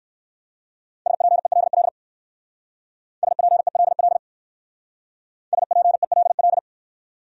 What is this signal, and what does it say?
Morse code, heard as a keyed tone, about 69 wpm, 700 Hz. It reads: SPELL SPELL SPELL